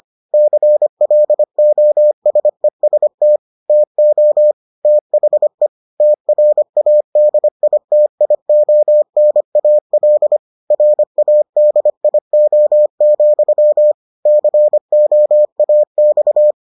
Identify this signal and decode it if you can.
Morse code, heard as a keyed tone, about 25 wpm, 600 Hz. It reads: CLOSEST TO THE TRADITIONAL RADIO, COAX